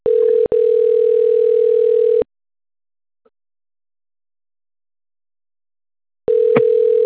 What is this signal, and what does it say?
Ringback tone (North America), 440 + 480 Hz.